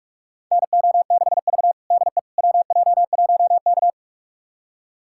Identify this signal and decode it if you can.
Morse code, heard as a keyed tone, about 46 wpm, 700 Hz. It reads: NO6V DE WJ1K
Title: USA Callsign DE Callsign - 4 Characters - 45wpm